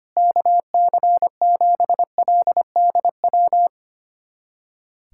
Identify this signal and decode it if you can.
Morse code, heard as a keyed tone, about 25 wpm, 700 Hz. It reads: KC7LDW
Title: USA Callsign DE Callsign - 6 Characters - 25wpm